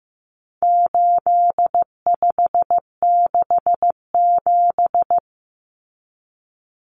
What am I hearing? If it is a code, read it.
Morse code, heard as a keyed tone, about 15 wpm, 700 Hz. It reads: 8567
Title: Numbers - 4-Digit 15wpm